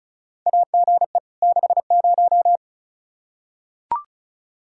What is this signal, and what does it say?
Morse code, heard as a keyed tone, about 36 wpm, 700 Hz. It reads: AGE 60
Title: Practice QSO - Exchange 3 - More Info - 35wpm